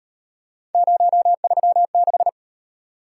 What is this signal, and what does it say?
Morse code, heard as a keyed tone, about 38 wpm, 700 Hz. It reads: 036